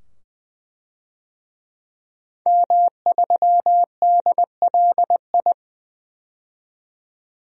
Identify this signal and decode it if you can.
Morse code, heard as a keyed tone, about 21 wpm, 700 Hz. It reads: M3DLI
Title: DXCC-100 Call Signs - 5 Characters - 20wpm